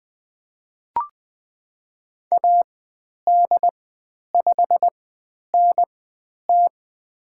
Morse code, heard as a keyed tone, about 20 wpm, spaced as for 10 wpm, 700 Hz. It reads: AD5NT
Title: 095 Lesson 16g - Letter c - Call Signs 20wpm